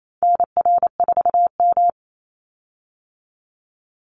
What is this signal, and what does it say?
Morse code, heard as a keyed tone, about 28 wpm, 700 Hz. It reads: NR4M
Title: Call Signs Repeated 2x - 28wpm